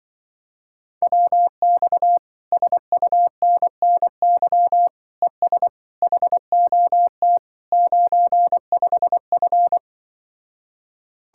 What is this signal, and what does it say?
Morse code, heard as a keyed tone, about 24 wpm, 700 Hz. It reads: WX SUNNY ES HOT 95F
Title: Speed-Racing Common QSO Phrases - 20wpm